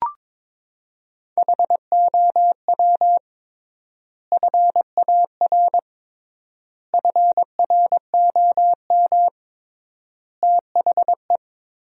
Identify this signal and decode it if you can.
Morse code, heard as a keyed tone, about 22 wpm, 700 Hz. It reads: HOW FAR FROM THE
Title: 3x Word-Spacing - Sentences from Top 400 Words - 22wpm